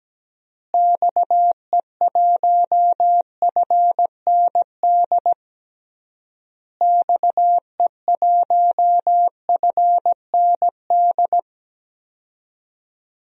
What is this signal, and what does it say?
Morse code, heard as a keyed tone, about 17 wpm, 700 Hz. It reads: XE1FND XE1FND
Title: DXCC-100 Call Signs - 6 Characters - Repeated 3x - 17wpm